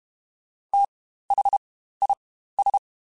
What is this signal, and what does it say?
Morse code, heard as a keyed tone, about 32 wpm, spaced as for 15 wpm, 800 Hz. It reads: THIS
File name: Spaced Characters / Spaced Characters-mp3s / Weather 4 24 sp @30